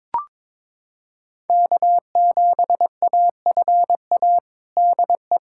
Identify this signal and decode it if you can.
Morse code, heard as a keyed tone, about 22 wpm, 700 Hz. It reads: K7AFA DE
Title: USA Callsign DE Callsign - 5 Characters - 22wpm